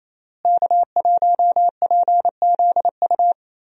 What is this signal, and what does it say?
Morse code, heard as a keyed tone, about 27 wpm, 700 Hz. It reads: K1PZU
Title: USA Call Signs - 5 Characters - Repeated 2x - 28wpm